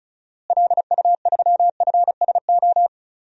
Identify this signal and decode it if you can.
Morse code, heard as a keyed tone, about 34 wpm, 700 Hz. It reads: LU3FSO
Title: DXCC-100 Call Signs - 6 Characters - Repeated 2x - 35wpm